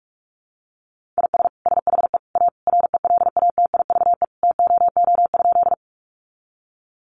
Morse code, heard as a keyed tone, about 44 wpm, 700 Hz. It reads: IS SHE A RELATIVE TOO?
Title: Sentences from Top 1400 Words 45wpm